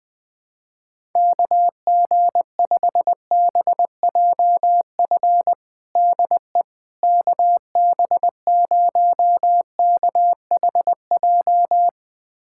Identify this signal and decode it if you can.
Morse code, heard as a keyed tone, about 20 wpm, 700 Hz. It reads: KG5BJF DE KB0KHJ
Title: USA Callsign DE Callsign - 6 Characters - 20wpm